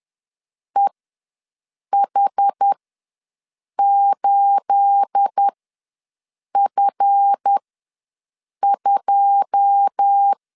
Morse code, heard as a keyed tone, about 10 wpm, spaced as for 6 wpm, 800 Hz. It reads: EH8F2
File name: LEZIONI / 10 SIGNS / 10B INT ws 10wpm 4134